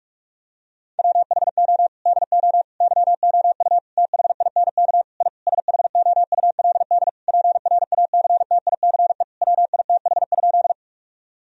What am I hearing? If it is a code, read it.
Morse code, heard as a keyed tone, about 46 wpm, 700 Hz. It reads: WHO DO YOU THINK I SHOULD PRACTICE WITH?
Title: Sentences from Top 1000 Words - Review Part 4 - 45wpm